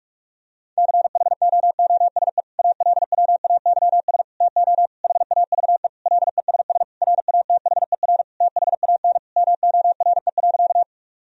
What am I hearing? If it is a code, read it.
Morse code, heard as a keyed tone, about 46 wpm, 700 Hz. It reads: CHOOSE ALWAYS TO HAVE LESS RATHER THAN MORE.
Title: Sentences from Top 1000 Words - Review Part 4 - 45wpm